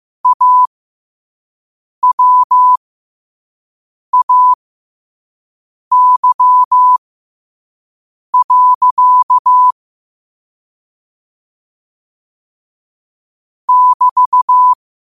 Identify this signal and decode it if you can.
Morse code, heard as a keyed tone, about 15 wpm, spaced as for 5 wpm, 1000 Hz. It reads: AWAY. <BT>
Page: New quotes every day in morse code at 5 Words per minute.